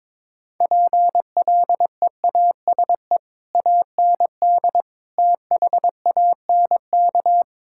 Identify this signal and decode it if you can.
Morse code, heard as a keyed tone, about 22 wpm, 700 Hz. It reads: PLEASE AND THANK
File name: Difficult Sentences with Hints - 22wpm